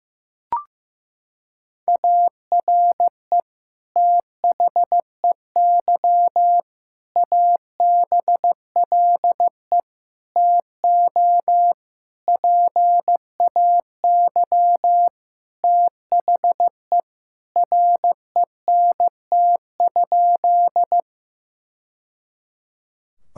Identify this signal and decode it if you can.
Morse code, heard as a keyed tone, about 15 wpm, 700 Hz. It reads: ARE THEY ABLE TO PAY THE RENT?
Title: Sentences from Top 1000 Words - Review Part 5 - 15wpm